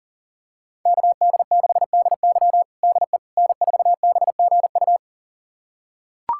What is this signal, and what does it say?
Morse code, heard as a keyed tone, about 37 wpm, 700 Hz. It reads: KD6DY DE N4BGU
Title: USA Callsign DE Callsign - 5 Characters - 40wpm